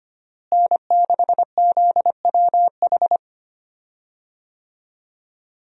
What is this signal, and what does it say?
Morse code, heard as a keyed tone, about 25 wpm, 700 Hz. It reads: N6ZWH
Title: USA Call Signs - 5 Characters - Repeated 2x - 25wpm